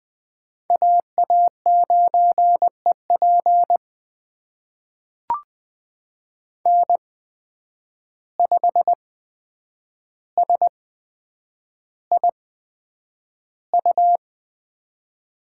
Morse code, heard as a keyed tone, about 20 wpm, spaced as for 9 wpm, 700 Hz. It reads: AA9EP N5SIU
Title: USA Call Signs - 5 Characters - 8x Character Spacing - 20wpm